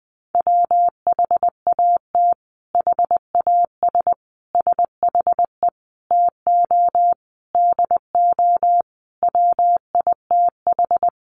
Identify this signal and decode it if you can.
Morse code, heard as a keyed tone, about 20 wpm, 700 Hz. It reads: WHAT HAS SHE TO DO WITH
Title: Sentences from Top 100 Words 20wpm